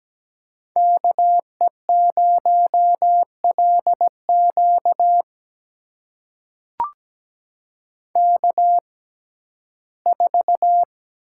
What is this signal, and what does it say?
Morse code, heard as a keyed tone, about 17 wpm, spaced as for 12 wpm, 700 Hz. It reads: KE0LQ K4
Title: USA Call Signs - 5 Characters - 6x Character Spacing - 17wpm